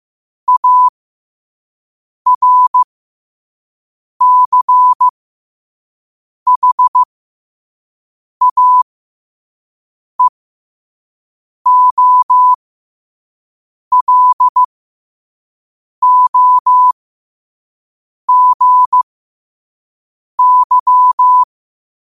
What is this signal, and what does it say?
Morse code, heard as a keyed tone, about 15 wpm, spaced as for 5 wpm, 1000 Hz. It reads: ARCHAEOLOGY
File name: qotdfiles / 5wpm